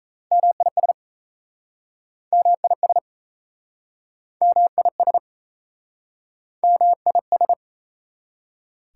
Morse code, heard as a keyed tone, about 40 wpm, 690 Hz. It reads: MIS MIS MIS MIS